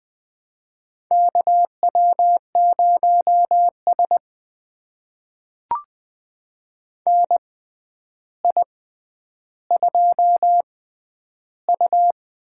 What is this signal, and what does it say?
Morse code, heard as a keyed tone, about 20 wpm, spaced as for 10 wpm, 700 Hz. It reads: KW0S NI2U
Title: USA Call Signs - 4 Characters - 6x Character Spacing - 20wpm